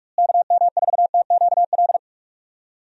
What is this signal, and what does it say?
Morse code, heard as a keyed tone, about 42 wpm, 700 Hz. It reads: KM4TQL
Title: USA Call Signs - 6 Characters - Repeated 3x - 45wpm